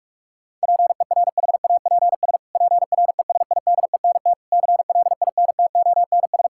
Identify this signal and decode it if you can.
Morse code, heard as a keyed tone, about 41 wpm, 700 Hz. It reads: PERHAPS PRESIDENT CLINTONS
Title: Pangram Challenge v2 - 45wpm